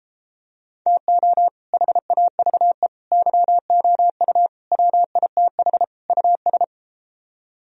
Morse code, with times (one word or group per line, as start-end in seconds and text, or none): 0.9-1.5 TO
1.7-2.9 HAVE
3.1-4.5 YOU
4.7-5.8 WITH
6.1-6.6 US